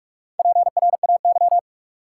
Morse code, heard as a keyed tone, about 46 wpm, 700 Hz. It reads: PRAY
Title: Four Letter Words - 45wpm